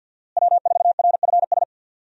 Morse code, heard as a keyed tone, about 50 wpm, 700 Hz. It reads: W4RFS
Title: USA Call Signs - 5 Characters - Repeated 2x - 50wpm